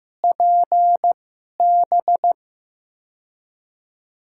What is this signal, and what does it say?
Morse code, heard as a keyed tone, about 15 wpm, spaced as for 11 wpm, 700 Hz. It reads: PB
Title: USA Call Signs - 6 Characters - 2x Character Spacing - 15wpm